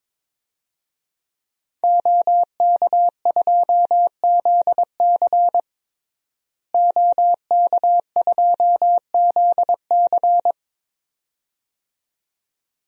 Morse code, heard as a keyed tone, about 22 wpm, 700 Hz. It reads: OK2ZC OK2ZC
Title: DXCC-100 Call Signs - 5 Characters - Repeated 2x - 22wpm